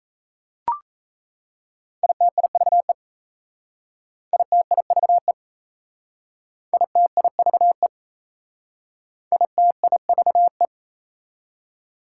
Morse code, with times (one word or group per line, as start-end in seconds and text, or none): 2.0-2.9 ITIVE
4.3-5.3 ITIVE
6.7-7.9 ITIVE
9.3-10.7 ITIVE